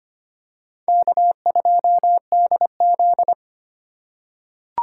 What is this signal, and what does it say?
Morse code, heard as a keyed tone, about 25 wpm, 700 Hz. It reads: K2DZ
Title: USA Call Signs - 3x Character Spacing - 25wpm